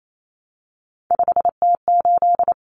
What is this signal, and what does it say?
Morse code, heard as a keyed tone, about 28 wpm, 700 Hz. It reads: 5T8
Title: Common Cut Numbers - 3-Digit 28wpm